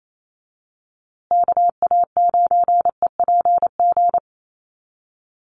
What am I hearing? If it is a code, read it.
Morse code, heard as a keyed tone, about 28 wpm, 700 Hz. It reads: KA9EPG
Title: USA Call Signs Repeated 3x 28wpm